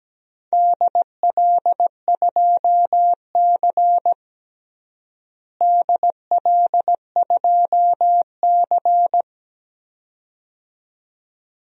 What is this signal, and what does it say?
Morse code, heard as a keyed tone, about 17 wpm, 700 Hz. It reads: DL2C DL2C
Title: DXCC-100 Call Signs - 4 Characters - Repeated 2x - 17wpm